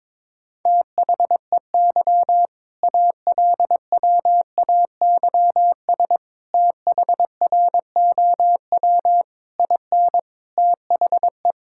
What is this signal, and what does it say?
Morse code, heard as a keyed tone, about 22 wpm, 690 Hz. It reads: THEY ALWAYS THROW IN THE
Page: morse-code-ninja-practice-22wpm